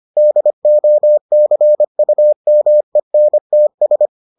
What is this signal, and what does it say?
Morse code, heard as a keyed tone, about 25 wpm, 600 Hz. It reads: DOCUMENTS